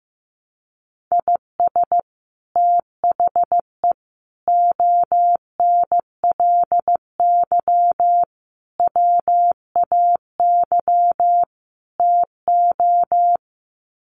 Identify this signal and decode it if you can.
Morse code, heard as a keyed tone, about 15 wpm, 700 Hz. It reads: IS THE ONLY WAY TO
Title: Sets of 5 Words Encore 15wpm